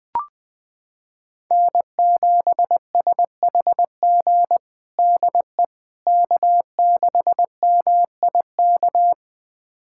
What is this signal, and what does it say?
Morse code, heard as a keyed tone, about 20 wpm, 700 Hz. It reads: N7SHG DE K6MIK
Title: USA Callsign DE Callsign - 5 Characters - 20wpm